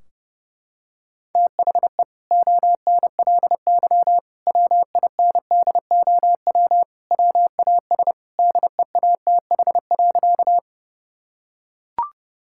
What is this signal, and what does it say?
Morse code, heard as a keyed tone, about 32 wpm, 700 Hz. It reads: THE ONLY WINDOW WAS DEATH.
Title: Sentences from Top 1600 Words 30wpm